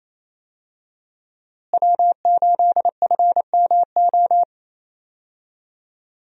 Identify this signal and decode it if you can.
Morse code, heard as a keyed tone, about 28 wpm, 700 Hz. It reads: W8FMO
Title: USA Call Signs - 5 Characters - Repeated 2x - 28wpm